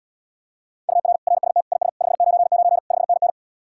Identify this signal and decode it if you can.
Morse code, heard as a keyed tone, about 80 wpm, 700 Hz. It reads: UR RST IS 599 5NN